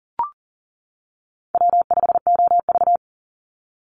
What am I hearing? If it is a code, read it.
Morse code, heard as a keyed tone, about 40 wpm, 700 Hz. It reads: W5OV
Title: Easy Call Signs - 40wpm